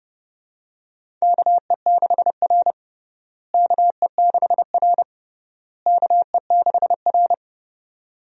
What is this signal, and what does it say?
Morse code, heard as a keyed tone, about 30 wpm, 700 Hz. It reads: KE6R KE6R KE6R
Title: USA Call Signs - 4 Characters - Repeated 3x - 30wpm